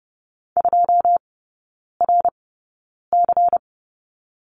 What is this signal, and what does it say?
Morse code, heard as a keyed tone, about 30 wpm, spaced as for 9 wpm, 690 Hz. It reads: 2RC
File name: Lesson 17e - Letter u - Three Characters 30wpm